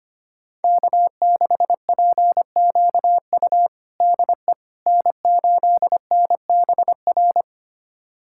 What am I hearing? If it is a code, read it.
Morse code, heard as a keyed tone, about 25 wpm, 700 Hz. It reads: K6PQU DE N8NBR
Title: USA Callsign DE Callsign - 5 Characters - 25wpm